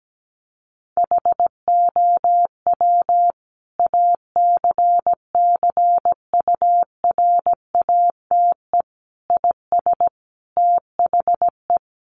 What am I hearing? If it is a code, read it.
Morse code, heard as a keyed tone, about 17 wpm, 700 Hz. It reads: HOW ACCURATE IS THE
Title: Sentences from Top 1200 Words 17wpm